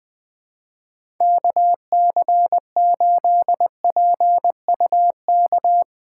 Morse code, heard as a keyed tone, about 20 wpm, 700 Hz. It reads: KC8PUK
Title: USA Call Signs - 6x Character Spacing - 20wpm